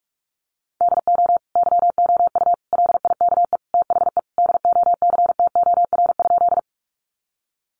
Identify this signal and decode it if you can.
Morse code, heard as a keyed tone, about 44 wpm, 700 Hz. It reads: DO YOU LIKE THE DOCTOR?
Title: Sentences from Top 1500 Words 45wpm